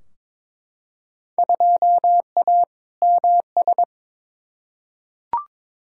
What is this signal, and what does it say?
Morse code, heard as a keyed tone, about 22 wpm, 700 Hz. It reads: 2A MS
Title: Field Day Exchanges - 22wpm